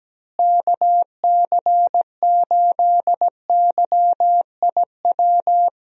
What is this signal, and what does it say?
Morse code, heard as a keyed tone, about 17 wpm, 700 Hz. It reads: KC8YIW
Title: USA Call Signs - 6 Characters - Repeated 3x - 17wpm